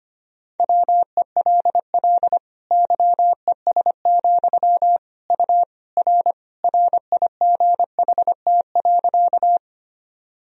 Morse code, heard as a keyed tone, about 25 wpm, 700 Hz. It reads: WELL YES, U R RIGHT.
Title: Sentences from Top 500 Words and Abbreviations 25wpm